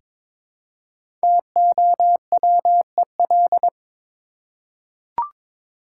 Morse code, heard as a keyed tone, about 22 wpm, 700 Hz. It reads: TOWEL
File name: 5 Letter Words - First 3 Letters Given - 22wpm